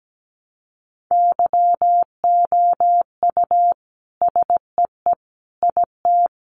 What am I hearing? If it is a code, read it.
Morse code, heard as a keyed tone, about 17 wpm, 700 Hz. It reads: YOU SEE IT
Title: Sentences from Top 200 Words 17wpm